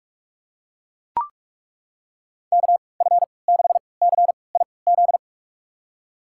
Morse code, heard as a keyed tone, about 42 wpm, spaced as for 25 wpm, 700 Hz. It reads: KF6CIZ
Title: USA Call Signs - 6 Characters - 3x Character Spacing - 45wpm